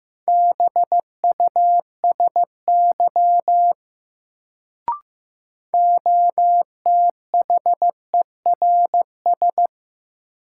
Morse code, heard as a keyed tone, about 15 wpm, 700 Hz. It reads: BUSY OTHERS
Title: Sentences from Top 1500 Words - New Word Drill Alt - 15wpm